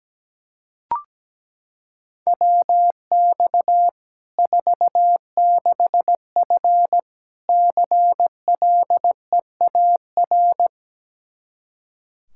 Morse code, heard as a keyed tone, about 17 wpm, 700 Hz. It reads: WX 46F CLEAR
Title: Practice QSO - Exchange 2 - Info - 17wpm